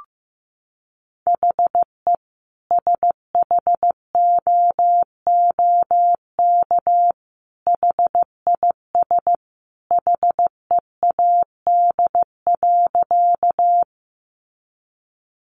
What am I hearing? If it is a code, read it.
Morse code, heard as a keyed tone, about 15 wpm, 700 Hz. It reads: HE SHOOK HIS HEAD.
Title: A Sound Of Thunder 15wpm